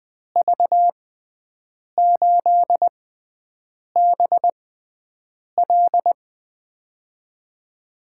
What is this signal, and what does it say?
Morse code, heard as a keyed tone, about 20 wpm, spaced as for 7 wpm, 700 Hz. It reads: V8BL
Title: USA Call Signs - 5 Characters - 6x Character Spacing - 20wpm